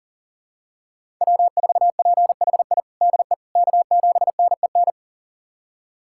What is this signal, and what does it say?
Morse code, heard as a keyed tone, about 40 wpm, 700 Hz. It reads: W4PHI DE K7NEN
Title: USA Callsign DE Callsign - 5 Characters - 40wpm